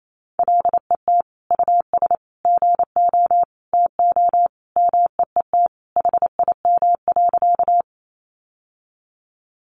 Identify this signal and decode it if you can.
Morse code, heard as a keyed tone, about 28 wpm, 700 Hz. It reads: LET US GO TO MEET HIM.
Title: Sentences from Top 1100 Words 28wpm